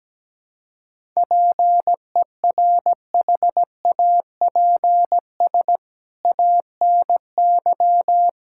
Morse code, heard as a keyed tone, about 17 wpm, 700 Hz. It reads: PERHAPS ANY
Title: Sentences from Top 1600 Words 17wpm